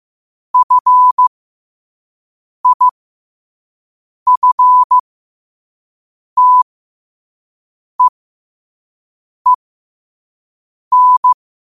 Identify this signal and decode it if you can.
Morse code, heard as a keyed tone, about 15 wpm, spaced as for 5 wpm, 1000 Hz. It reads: FIFTEEN
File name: qotdfiles / 5wpm